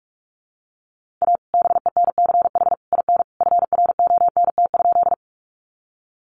Morse code, heard as a keyed tone, about 44 wpm, 700 Hz. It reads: A BENCH IN FRONT?
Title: Sentences from Top 1400 Words 45wpm